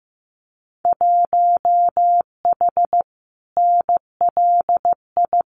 Morse code, heard as a keyed tone, about 15 wpm, 700 Hz. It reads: 1H NLI
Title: Winter Field Day Exchanges 15wpm